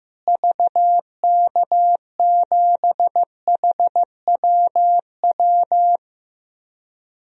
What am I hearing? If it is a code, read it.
Morse code, heard as a keyed tone, about 15 wpm, 700 Hz. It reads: VK7HWW
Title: DXCC-100 Call Signs - 6 Characters - Repeated 3x - 15wpm